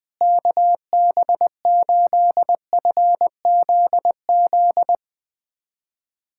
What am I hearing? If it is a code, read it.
Morse code, heard as a keyed tone, about 20 wpm, 700 Hz. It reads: KB8FZZ
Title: USA Callsign DE Callsign 20wpm